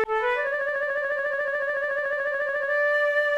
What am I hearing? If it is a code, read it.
There is no code.